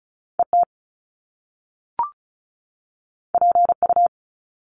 Morse code, heard as a keyed tone, about 36 wpm, 700 Hz. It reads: ET PU
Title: Top Two Letter Combinations in Words 35wpm